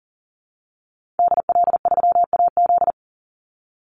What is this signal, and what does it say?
Morse code, heard as a keyed tone, about 40 wpm, 700 Hz. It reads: DL3AZ
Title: Call Signs Repeated 3x - 40wpm